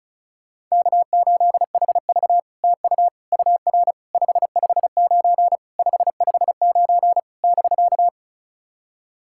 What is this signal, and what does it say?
Morse code, heard as a keyed tone, about 36 wpm, 700 Hz. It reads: K8HV TU UR 559 559 <BK>
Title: SOTA QSO Practice 35wpm